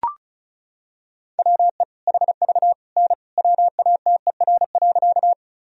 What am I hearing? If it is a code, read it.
Morse code, heard as a keyed tone, about 36 wpm, 700 Hz. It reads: WE HV N WATER.
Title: Sentences from Top 100 Words and Abbreviations 35wpm